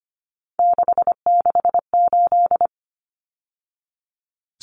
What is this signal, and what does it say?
Morse code, heard as a keyed tone, about 25 wpm, 700 Hz. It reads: 668
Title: Numbers - 3-Digit 25wpm